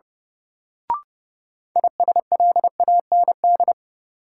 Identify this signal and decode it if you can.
Morse code, heard as a keyed tone, about 30 wpm, 700 Hz. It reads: ISLAND
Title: Sentences from Top 500 Words - New Word Drill - 30wpm